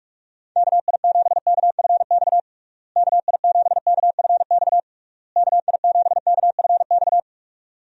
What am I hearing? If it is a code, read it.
Morse code, heard as a keyed tone, about 42 wpm, 700 Hz. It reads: KI7KFX KI7KFX KI7KFX
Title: USA Call Signs - 6 Characters - Repeated 3x - 45wpm